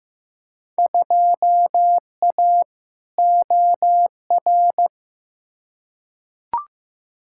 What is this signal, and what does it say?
Morse code, heard as a keyed tone, about 15 wpm, 700 Hz. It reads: 2A OR
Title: Field Day Exchanges Encore - 15wpm